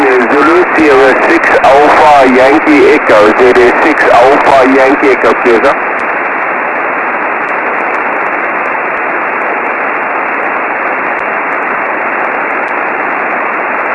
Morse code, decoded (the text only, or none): none